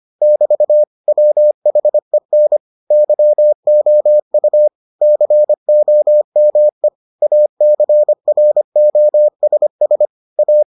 Morse code, heard as a keyed tone, about 25 wpm, 600 Hz. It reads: <BT> WHEN YOU COME ACROSS A